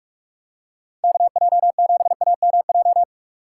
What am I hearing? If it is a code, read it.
Morse code, heard as a keyed tone, about 43 wpm, 700 Hz. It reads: KJ7AMJ
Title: USA Call Signs - 6 Characters - 2x Character Spacing - 45wpm